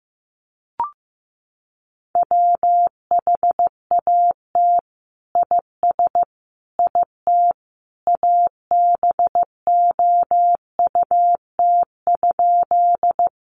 Morse code, heard as a keyed tone, about 15 wpm, 700 Hz. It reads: WHAT IS IT ABOUT?
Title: Sentences from Top 100 Words 15wpm